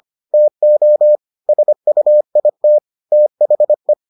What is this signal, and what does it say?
Morse code, heard as a keyed tone, about 25 wpm, 600 Hz. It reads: TO SUIT THE